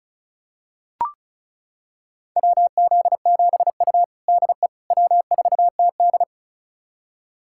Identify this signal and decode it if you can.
Morse code, heard as a keyed tone, about 36 wpm, 700 Hz. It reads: WZ7U DE W4TD
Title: USA Callsign DE Callsign - 4 Characters - 35wpm